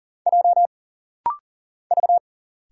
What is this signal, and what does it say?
Morse code, heard as a keyed tone, about 40 wpm, 700 Hz. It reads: J V
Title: Single Letters - Rapid-Fire 40wpm